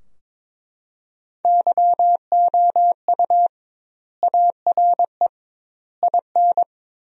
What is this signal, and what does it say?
Morse code, heard as a keyed tone, about 22 wpm, 700 Hz. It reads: YOU ARE IN
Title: 2x Word-Spacing - Sentences from Top 500 Words - 22wpm